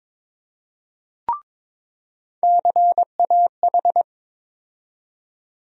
Morse code, heard as a keyed tone, about 22 wpm, 700 Hz. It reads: CAH
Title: USA Call Sign Postfixes - 3 Letters 22wpm